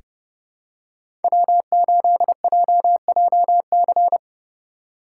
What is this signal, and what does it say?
Morse code, heard as a keyed tone, about 30 wpm, 700 Hz. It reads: W8JJC
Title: USA Call Signs - 5 Characters - 3x Character Spacing - 30wpm